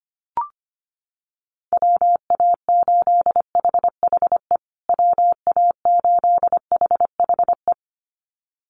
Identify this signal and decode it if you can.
Morse code, heard as a keyed tone, about 25 wpm, 700 Hz. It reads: WA8HHE WA8HHE
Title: USA Call Signs Repeated 2x 25wpm